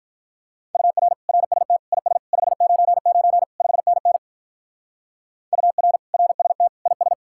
Morse code, heard as a keyed tone, about 43 wpm, 700 Hz. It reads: UR RST IS 599 5NN UR RST IS
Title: Speed-Racing Common QSO Phrases - 35wpm